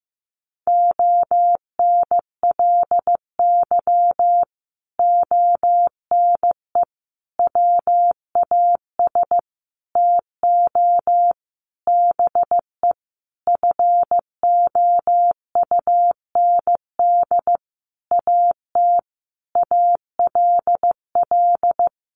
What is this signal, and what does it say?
Morse code, heard as a keyed tone, about 14 wpm, 700 Hz. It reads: ONLY ONE WAS TO BE FOUND AT ALL
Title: Sentences from Top 1400 Words 15wpm